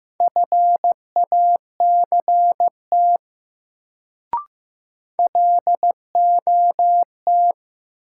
Morse code, heard as a keyed tone, about 15 wpm, 700 Hz. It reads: FACT LOT
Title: Sets of 2 Words - New Word Drill - 15wpm